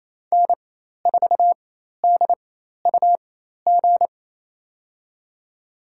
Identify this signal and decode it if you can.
Morse code, heard as a keyed tone, about 28 wpm, spaced as for 13 wpm, 700 Hz. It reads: N4DUG
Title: USA Call Signs - 4x Character Spacing - 28wpm